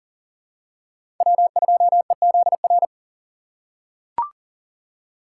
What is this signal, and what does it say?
Morse code, heard as a keyed tone, about 40 wpm, 700 Hz. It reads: W2EZR
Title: USA Call Signs - 5 Characters - 3x Character Spacing - 40wpm